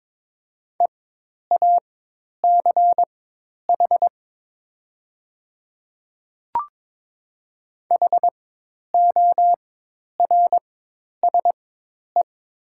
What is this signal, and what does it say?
Morse code, heard as a keyed tone, about 22 wpm, spaced as for 10 wpm, 700 Hz. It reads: EACH HORSE
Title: 4x Character Spacing - Top 300 Words - 22wpm